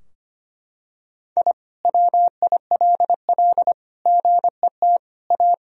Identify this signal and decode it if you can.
Morse code, heard as a keyed tone, about 25 wpm, 700 Hz. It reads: I WILL GET A
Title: Sentences from Top 1000 Words - Review Part 2 - 25wpm